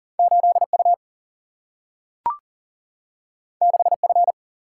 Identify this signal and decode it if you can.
Morse code, heard as a keyed tone, about 42 wpm, 700 Hz. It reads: 8U 6F
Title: Single-Letter Single-Number 40wpm